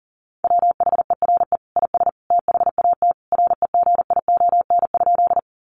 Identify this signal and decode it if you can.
Morse code, heard as a keyed tone, about 40 wpm, 700 Hz. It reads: WHERE IS THAT REGION?
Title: Sentences from Top 1500 Words 40wpm